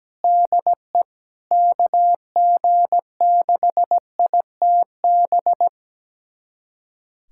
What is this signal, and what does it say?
Morse code, heard as a keyed tone, about 17 wpm, 700 Hz. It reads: DE KG6ITB
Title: USA Callsign DE Callsign 17wpm